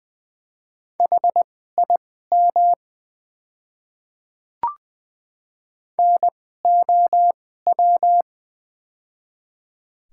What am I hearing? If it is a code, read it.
Morse code, heard as a keyed tone, about 20 wpm, spaced as for 14 wpm, 700 Hz. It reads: HIM NOW
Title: 2x Character Spacing - Top 500 Words - 20wpm